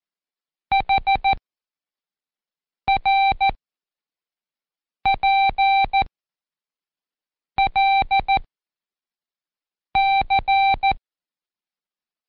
Morse code, heard as a keyed tone, about 14 wpm, 780 Hz. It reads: H R P L C
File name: Code Quick / AudioDownld1 / Seg40-Day5-CheckTest 5